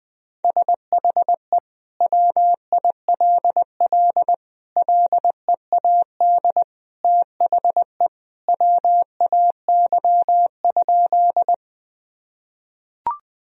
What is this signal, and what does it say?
Morse code, heard as a keyed tone, about 21 wpm, 700 Hz. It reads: SHE WILL LEAD THE WAY?
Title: Sentences from Top 1000 Words - Review Part 2 - 20wpm